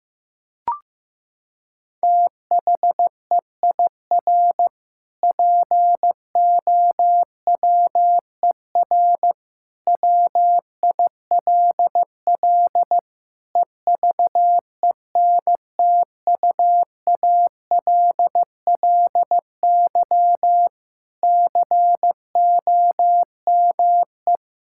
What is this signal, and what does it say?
Morse code, heard as a keyed tone, about 15 wpm, 700 Hz. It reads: THEIR POWER WILL EVENTUALLY COME
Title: Sentences from Top 1000 Words - Review Part 6 - 15wpm